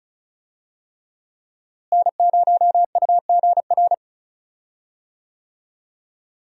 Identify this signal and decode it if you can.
Morse code, heard as a keyed tone, about 36 wpm, 700 Hz. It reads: N0UGR
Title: USA Call Signs - 5 Characters - Repeated 3x - 35wpm